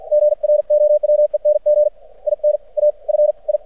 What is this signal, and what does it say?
Morse code, heard as a keyed tone, about 52 wpm, spaced as for 44 wpm, 600 Hz. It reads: GROWENG IN A FI
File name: archived / CRSnet